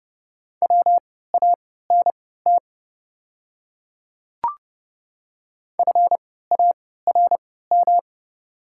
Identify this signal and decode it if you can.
Morse code, heard as a keyed tone, about 31 wpm, spaced as for 17 wpm, 700 Hz. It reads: WANT FARM
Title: 3x Character Spacing - Top 500 Words - 30wpm